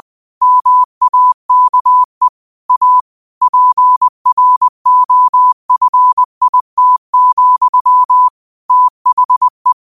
Morse code, heard as a keyed tone, about 20 wpm, 1000 Hz. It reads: MAKE A PROFIT, THE
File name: 20wpm